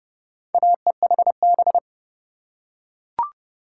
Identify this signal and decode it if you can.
Morse code, heard as a keyed tone, about 28 wpm, 700 Hz. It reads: AEHB